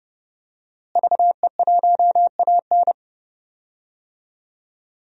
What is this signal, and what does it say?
Morse code, heard as a keyed tone, about 30 wpm, 700 Hz. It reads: VE1AN
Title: DXCC-100 Call Signs - 5 Characters - Repeated 2x - 30wpm